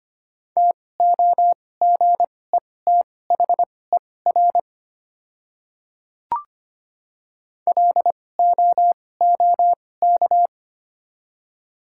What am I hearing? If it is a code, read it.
Morse code, heard as a keyed tone, about 25 wpm, spaced as for 18 wpm, 700 Hz. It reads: TOGETHER LOOK
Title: 2x Character Spacing - Top 500 Words - 25wpm